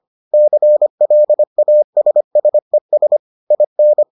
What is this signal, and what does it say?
Morse code, heard as a keyed tone, about 25 wpm, 600 Hz. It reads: CLASSES IN